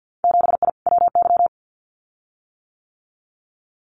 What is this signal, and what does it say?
Morse code, heard as a keyed tone, about 50 wpm, 700 Hz. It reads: 7I WY
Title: Winter Field Day Exchanges Encore 50wpm